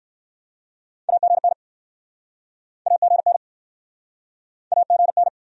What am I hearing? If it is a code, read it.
Morse code, heard as a keyed tone, about 60 wpm, 700 Hz. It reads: AGN AGN AGN